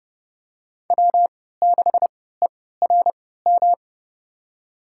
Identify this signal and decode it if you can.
Morse code, heard as a keyed tone, about 30 wpm, spaced as for 17 wpm, 700 Hz. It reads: W6ERM
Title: USA Call Signs - 5 Characters - 3x Character Spacing - 30wpm